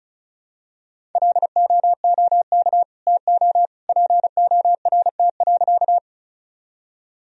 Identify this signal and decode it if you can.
Morse code, heard as a keyed tone, about 34 wpm, 700 Hz. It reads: LOOK TO PORT.
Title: Sentences from Top 1000 Words - Review Part 1 - 35wpm